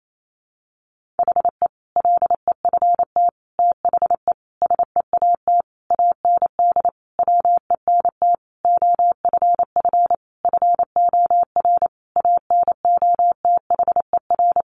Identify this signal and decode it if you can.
Morse code, heard as a keyed tone, about 28 wpm, 700 Hz. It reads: HE LEFT THE SEAT AND WENT OFF FOR ANOTHER
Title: Sentences from Top 1500 Words 28wpm